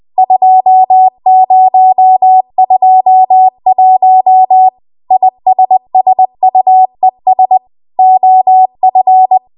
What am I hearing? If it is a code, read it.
Morse code, heard as a keyed tone, about 20 wpm, 750 Hz. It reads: 2021 ISSUES OF